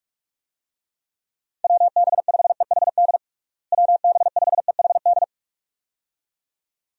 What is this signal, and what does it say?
Morse code, heard as a keyed tone, about 46 wpm, 700 Hz. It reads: WB5EHD WB5EHD
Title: USA Call Signs - 6 Characters - Repeated 2x - 45wpm